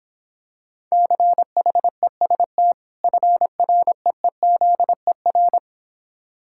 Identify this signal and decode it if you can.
Morse code, heard as a keyed tone, about 26 wpm, 700 Hz. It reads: CHEST FREEZER